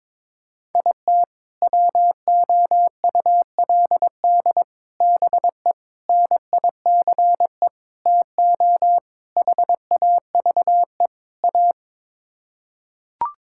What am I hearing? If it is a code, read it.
Morse code, heard as a keyed tone, about 22 wpm, 700 Hz. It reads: IT WOULD BE NICE TO HAVE A
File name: practice / 22wpm / Sets of 7 Words - 22wpm